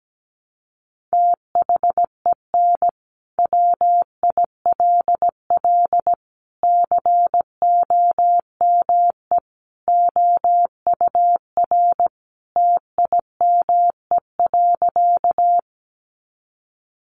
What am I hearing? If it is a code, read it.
Morse code, heard as a keyed tone, about 17 wpm, 700 Hz. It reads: THEN WILL COME OUR TIME.
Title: Sentences from Top 200 Words 17wpm